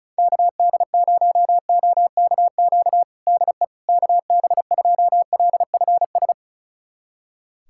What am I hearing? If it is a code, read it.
Morse code, heard as a keyed tone, about 35 wpm, 700 Hz. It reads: KD0OKQ DE KB2LFS
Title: USA Callsign DE Callsign - 6 Characters - 35wpm